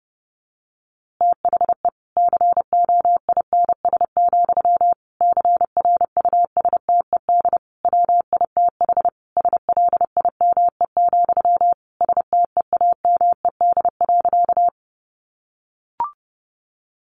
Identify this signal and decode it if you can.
Morse code, heard as a keyed tone, about 30 wpm, 700 Hz. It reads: THE COINS, CRUSTED WITH SLIME, STEAMED.
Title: A Sound Of Thunder 30wpm